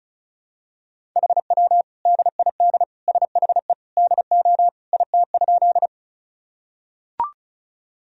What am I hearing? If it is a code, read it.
Morse code, heard as a keyed tone, about 36 wpm, 700 Hz. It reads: HW DID SHE DO IT?
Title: Sentences from Top 100 Words and Abbreviations 35wpm